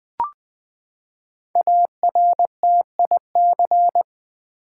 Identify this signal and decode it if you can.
Morse code, heard as a keyed tone, about 20 wpm, 700 Hz. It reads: ARTIC
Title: Build A Word - 5 Letters At A Time - 10-Letter Words - 20wpm